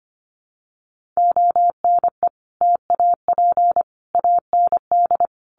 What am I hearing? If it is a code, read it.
Morse code, heard as a keyed tone, about 25 wpm, 700 Hz. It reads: ONE TAP AND
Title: Sentences from Top 1400 Words 25wpm